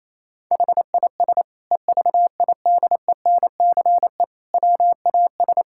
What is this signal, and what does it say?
Morse code, heard as a keyed tone, about 28 wpm, 700 Hz. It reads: HIS EVIDENCE WAS
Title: Sentences from Top 1600 Words 28wpm